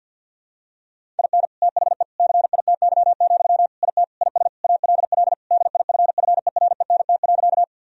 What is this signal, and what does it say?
Morse code, heard as a keyed tone, about 51 wpm, 700 Hz. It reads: IN THE CITY, IT IS ALL DIFFERENT.
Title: Sentences from Top 1000 Words - Review Part 3 - 50wpm